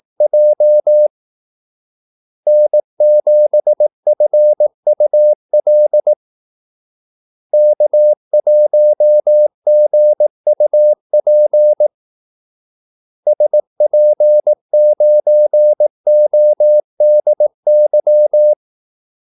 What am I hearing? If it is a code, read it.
Morse code, heard as a keyed tone, about 18 wpm, 600 Hz. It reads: J N7FUL K1GUP SP9ODY
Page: Kaldesignaler 18 wpm